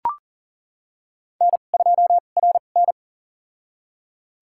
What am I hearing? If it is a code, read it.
Morse code, heard as a keyed tone, about 37 wpm, spaced as for 28 wpm, 700 Hz. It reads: N2RN
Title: USA Call Signs - 2x Character Spacing - 40wpm